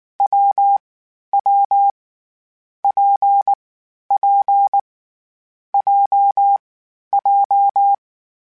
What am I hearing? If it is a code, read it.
Morse code, heard as a keyed tone, about 19 wpm, 800 Hz. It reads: W W P P J J
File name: Variable Speed / Variable Speed-mp3s / WPJ WPJ1 sp @18